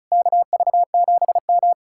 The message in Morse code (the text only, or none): KV7M